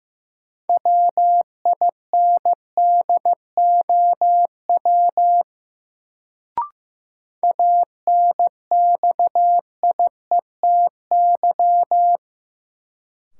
Morse code, heard as a keyed tone, about 15 wpm, 700 Hz. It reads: WINDOW ANXIETY
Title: Sentences from Top 1600 Words - New Word Drill - 15wpm